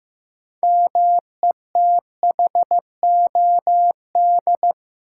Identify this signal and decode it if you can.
Morse code, heard as a keyed tone, about 15 wpm, 700 Hz. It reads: METHOD
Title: Six Letter Words - 15wpm